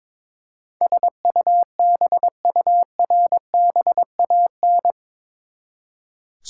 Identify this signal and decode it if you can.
Morse code, heard as a keyed tone, about 22 wpm, 700 Hz. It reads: SUBURBAN
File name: practice / 22wpm / Eight Letter Words - 22wpm